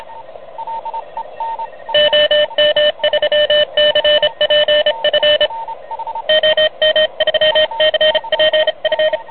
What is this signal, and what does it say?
Morse code, heard as a keyed tone, about 30 wpm, 590 Hz, with radio noise. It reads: OM3CPF OM3CPF
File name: truSDX / MWC29.8